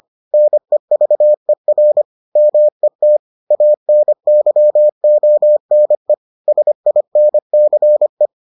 Morse code, heard as a keyed tone, about 25 wpm, 600 Hz. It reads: NEVER MET ANYONE SINCE